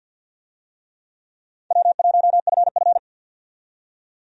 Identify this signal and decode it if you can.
Morse code, heard as a keyed tone, about 52 wpm, 700 Hz. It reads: W1FF
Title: USA Call Signs - 4 Characters - 50wpm